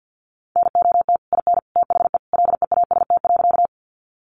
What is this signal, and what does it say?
Morse code, heard as a keyed tone, about 50 wpm, 700 Hz. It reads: NOT IN THE LEAST.
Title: Sentences from Top 700 Words 50wpm